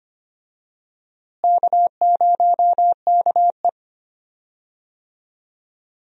Morse code, heard as a keyed tone, about 25 wpm, 700 Hz. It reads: K0KE
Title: USA Call Signs - 4 Characters - Repeated 2x - 25wpm